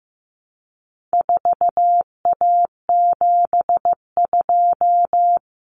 Morse code, heard as a keyed tone, about 15 wpm, 700 Hz. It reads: 4A72
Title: Common Cut Numbers - 4-Digit 15wpm